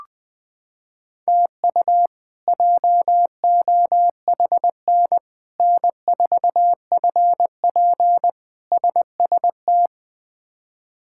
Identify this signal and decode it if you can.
Morse code, heard as a keyed tone, about 20 wpm, 700 Hz. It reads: TU JOHN N4FP SST
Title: K1USN SST QSO Practice - 20wpm